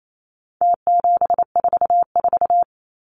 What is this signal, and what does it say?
Morse code, heard as a keyed tone, about 28 wpm, 700 Hz. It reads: T744
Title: Numbers - Contest Style - 4-Digit 28wpm